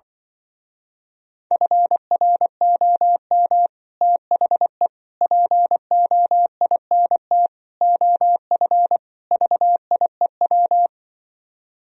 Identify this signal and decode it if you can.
Morse code, heard as a keyed tone, about 24 wpm, 700 Hz. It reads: FROM THE POINT OF VIEW
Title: Speed-Racing - Sets of 5 Words 20wpm